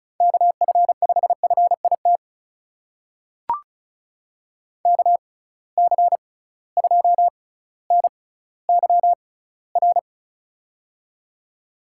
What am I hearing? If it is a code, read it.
Morse code, heard as a keyed tone, about 36 wpm, spaced as for 18 wpm, 700 Hz. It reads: KF5FIT KC2NYR
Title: USA Call Signs - 6 Characters - 6x Character Spacing - 35wpm